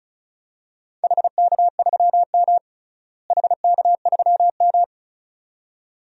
Morse code, heard as a keyed tone, about 36 wpm, 700 Hz. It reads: HK3M HK3M
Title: DXCC-100 Call Signs - 4 Characters - Repeated 2x - 35wpm